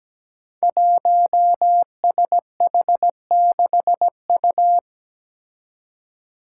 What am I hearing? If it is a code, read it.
Morse code, heard as a keyed tone, about 17 wpm, 700 Hz. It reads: 1SH6U
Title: Five Challenging Letter-Number Combinations 17wpm